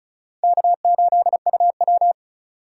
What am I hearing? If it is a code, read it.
Morse code, heard as a keyed tone, about 36 wpm, 700 Hz. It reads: K8UW
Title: USA Call Signs - 4 Characters - 2x Character Spacing - 35wpm